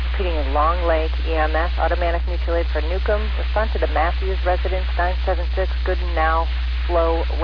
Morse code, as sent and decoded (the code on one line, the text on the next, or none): none